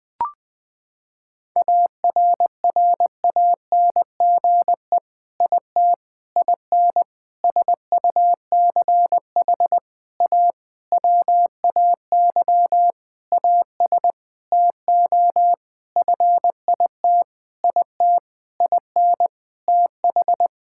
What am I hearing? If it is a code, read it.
Morse code, heard as a keyed tone, about 21 wpm, 700 Hz. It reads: ARRANGE IT IN SUCH A WAY AS TO FIT IT IN TH
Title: Sentences with 6-Word Phrases - 20wpm